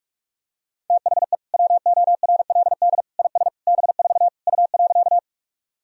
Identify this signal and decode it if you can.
Morse code, heard as a keyed tone, about 46 wpm, 700 Hz. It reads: THE WORLD IS B4 U.
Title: Sentences from Top 200 Words and Abbreviations 45wpm